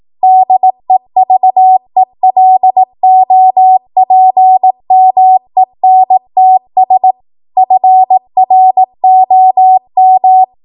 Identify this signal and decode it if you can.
Morse code, heard as a keyed tone, about 18 wpm, 750 Hz. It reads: DEVELOPMENTS FROM
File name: Morse / 18WPM